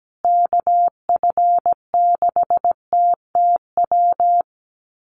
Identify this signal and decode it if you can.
Morse code, heard as a keyed tone, about 17 wpm, 700 Hz. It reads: KF6TTW
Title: USA Call Signs Repeated 3x 17wpm